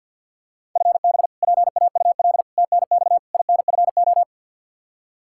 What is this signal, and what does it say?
Morse code, heard as a keyed tone, about 51 wpm, 700 Hz. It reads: FB PAUL TNX INFO
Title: Practice QSO - Exchange 3 - More Info - 50wpm